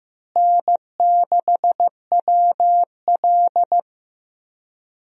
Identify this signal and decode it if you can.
Morse code, heard as a keyed tone, about 15 wpm, 700 Hz. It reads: N6WL
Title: USA Call Signs - 4 Characters - Repeated 3x - 15wpm